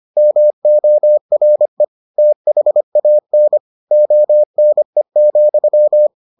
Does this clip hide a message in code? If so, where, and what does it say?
Yes: Morse code from 0.2 to 6.1 s, heard as a keyed tone, about 25 wpm, 600 Hz. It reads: MORE THAN ONE,